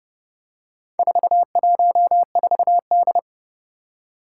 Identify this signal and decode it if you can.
Morse code, heard as a keyed tone, about 30 wpm, 700 Hz. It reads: 414D
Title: Four Challenging Letter-Number Combinations 30wpm